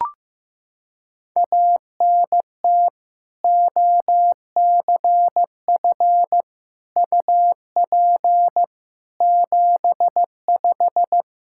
Morse code, heard as a keyed tone, about 16 wpm, 700 Hz. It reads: ANT OCF UP 75
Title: Practice QSO - Exchange 2 - Info - 15wpm